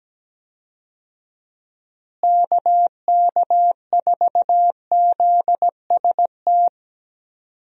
Morse code, heard as a keyed tone, about 17 wpm, 700 Hz. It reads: KK4ZST
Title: USA Call Signs - 6 Characters - Repeated 3x - 17wpm